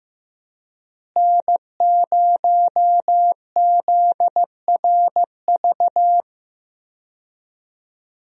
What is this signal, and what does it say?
Morse code, heard as a keyed tone, about 15 wpm, 700 Hz. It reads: N0ZRV
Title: USA Call Signs - 5 Characters - Repeated 3x - 15wpm